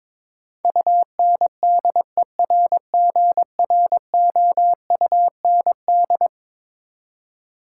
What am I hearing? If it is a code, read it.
Morse code, heard as a keyed tone, about 22 wpm, 700 Hz. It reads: UNDERGROUND
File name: practice / 22wpm / Eleven Letter Words - 22wpm